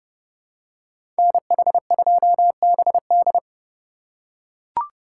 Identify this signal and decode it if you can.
Morse code, heard as a keyed tone, about 31 wpm, 700 Hz. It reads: NH2BD
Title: USA Call Signs - 5 Characters - 3x Character Spacing - 30wpm